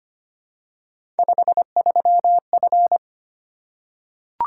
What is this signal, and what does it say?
Morse code, heard as a keyed tone, about 25 wpm, 700 Hz. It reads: 53F
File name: Three Letter-Number Combinations - 25wpm